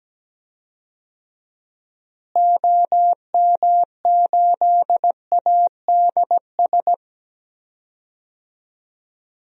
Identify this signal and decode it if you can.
Morse code, heard as a keyed tone, about 17 wpm, 700 Hz. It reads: OM8ADS
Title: DXCC-100 Call Signs - 6 Characters - Repeated 3x - 17wpm